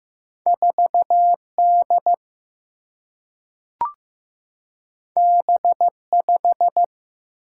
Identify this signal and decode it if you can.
Morse code, heard as a keyed tone, about 15 wpm, 700 Hz. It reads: 4D B5
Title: Single-Letter Single-Number 15wpm